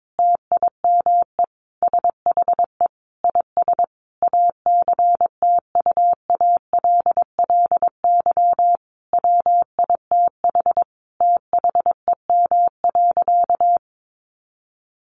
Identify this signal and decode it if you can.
Morse code, heard as a keyed tone, about 22 wpm, 700 Hz. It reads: TIME SHE IS ACTUALLY WITH THEM.
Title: Sentences from Top 700 Words 22wpm